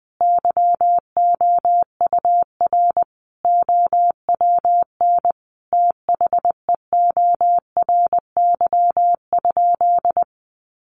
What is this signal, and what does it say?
Morse code, heard as a keyed tone, about 20 wpm, 700 Hz. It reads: YOUR OWN THEORY?
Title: Sentences from Top 1000 Words 20wpm